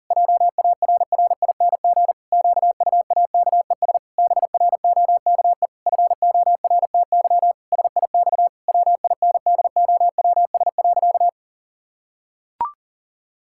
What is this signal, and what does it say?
Morse code, heard as a keyed tone, about 40 wpm, 700 Hz. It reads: JARRING QUAKES BROKE FORTY SIX WINDOWS.
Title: Pangram Challenge v2 - 40wpm